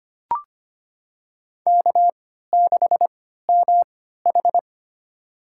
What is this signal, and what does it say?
Morse code, heard as a keyed tone, about 25 wpm, spaced as for 14 wpm, 700 Hz. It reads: K6MH
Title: USA Call Signs - 4 Characters - 3x Character Spacing - 25wpm